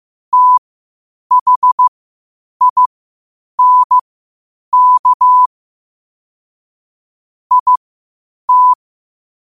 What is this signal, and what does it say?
Morse code, heard as a keyed tone, about 15 wpm, spaced as for 8 wpm, 1000 Hz. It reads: THINK IT
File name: 8wpm